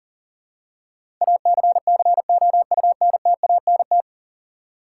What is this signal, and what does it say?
Morse code, heard as a keyed tone, about 40 wpm, 700 Hz. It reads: ACCOUNTANT
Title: Practice QSO - Exchange 3 - More Info - 40wpm